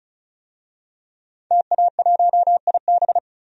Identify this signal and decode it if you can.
Morse code, heard as a keyed tone, about 34 wpm, 700 Hz. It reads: TA1IB
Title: DXCC-100 Call Signs - 5 Characters - 35wpm